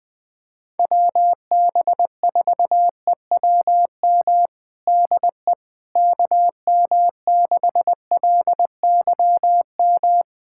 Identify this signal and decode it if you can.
Morse code, heard as a keyed tone, about 21 wpm, 700 Hz. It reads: WB4EWM DE KM6LYM
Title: USA Callsign DE Callsign 20wpm